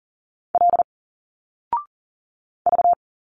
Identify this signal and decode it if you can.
Morse code, heard as a keyed tone, about 40 wpm, 700 Hz. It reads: L V
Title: Single Letters 40wpm